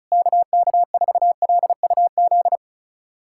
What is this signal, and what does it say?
Morse code, heard as a keyed tone, about 34 wpm, 700 Hz. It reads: KK4LUZ
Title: USA Callsign DE Callsign 35wpm